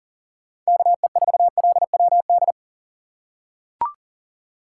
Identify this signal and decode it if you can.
Morse code, heard as a keyed tone, about 38 wpm, 700 Hz. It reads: KE4LWD
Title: USA Call Signs - 6 Characters - 2x Character Spacing - 40wpm